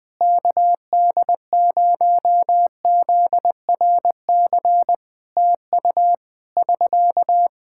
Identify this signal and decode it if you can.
Morse code, heard as a keyed tone, about 21 wpm, 700 Hz. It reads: KD0ZRC TU <SK>
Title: Practice QSO - Exchange 4 - Ending - 20wpm